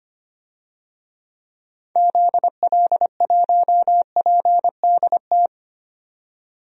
Morse code, heard as a keyed tone, about 25 wpm, 700 Hz. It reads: ZL1PDT
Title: DXCC-100 Call Signs - 6 Characters - 25wpm